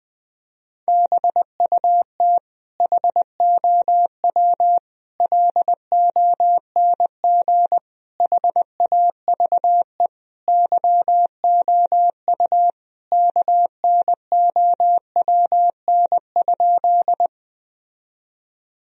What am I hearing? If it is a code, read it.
Morse code, heard as a keyed tone, about 20 wpm, 700 Hz. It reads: BUT HOW LONG HAVE YOU KNOWN?
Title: Sentences from Top 1000 Words - Review Part 3 - 20wpm